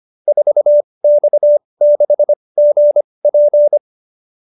Morse code, heard as a keyed tone, about 25 wpm, spaced as for 20 wpm, 570 Hz. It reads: 4X6GP
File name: QSO310 20